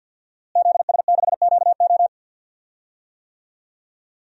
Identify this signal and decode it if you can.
Morse code, heard as a keyed tone, about 46 wpm, 700 Hz. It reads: ZS6QO
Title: DXCC-100 Call Signs - 5 Characters - Repeated 3x - 50wpm